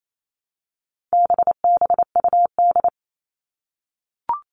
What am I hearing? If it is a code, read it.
Morse code, heard as a keyed tone, about 28 wpm, 700 Hz. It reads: BBUD
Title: All Cut Numbers - 4-Digit 28wpm